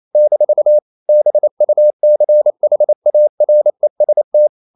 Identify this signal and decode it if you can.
Morse code, heard as a keyed tone, about 28 wpm, 600 Hz. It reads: - BUCHAREST
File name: CW / Practice / NationalCapitals28